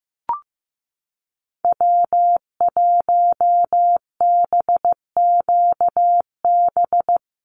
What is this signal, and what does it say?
Morse code, heard as a keyed tone, about 15 wpm, 700 Hz. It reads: W1BQB
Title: USA Call Signs Repeated 2x 15wpm